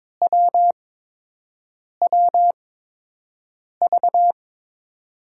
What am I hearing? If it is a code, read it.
Morse code, heard as a keyed tone, about 21 wpm, spaced as for 6 wpm, 700 Hz. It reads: WWV
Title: USA Call Signs - 8x Character Spacing - 22wpm